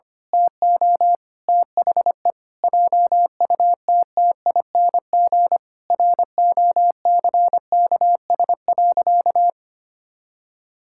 Morse code, heard as a keyed tone, about 25 wpm, 700 Hz. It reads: TO THE JUTTING ROCKS.
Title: Pangram Challenge v2 - 25wpm